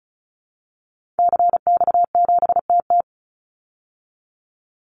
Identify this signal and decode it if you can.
Morse code, heard as a keyed tone, about 34 wpm, 700 Hz. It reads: CX7TT
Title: Call Signs Repeated 3x - 35wpm